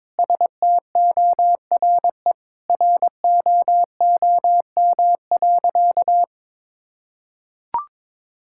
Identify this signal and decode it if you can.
Morse code, heard as a keyed tone, about 22 wpm, 700 Hz. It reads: STORE ROOM.
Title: Sentences from Top 1000 Words - Review Part 5 - 22wpm